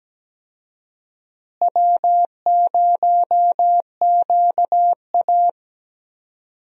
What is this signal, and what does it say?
Morse code, heard as a keyed tone, about 17 wpm, 700 Hz. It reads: W0QA
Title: USA Call Signs - 4 Characters - Repeated 3x - 17wpm